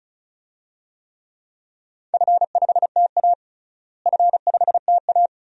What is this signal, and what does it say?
Morse code, heard as a keyed tone, about 34 wpm, 700 Hz. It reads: F5TA F5TA
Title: DXCC-100 Call Signs - 4 Characters - Repeated 2x - 35wpm